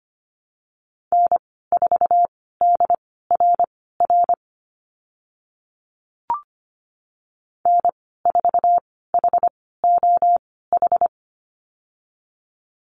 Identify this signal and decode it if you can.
Morse code, heard as a keyed tone, about 25 wpm, spaced as for 16 wpm, 700 Hz. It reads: N4DRR N4HOH
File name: Lesson 15g - Number 5 - Call Signs 25wpm